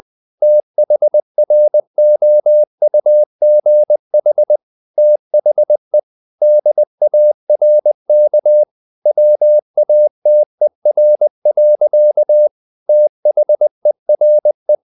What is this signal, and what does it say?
Morse code, heard as a keyed tone, about 20 wpm, 600 Hz. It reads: THROUGH THE DARK WATER. THERE
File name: the old man and the sea / the old man and the sea part 1 20wpm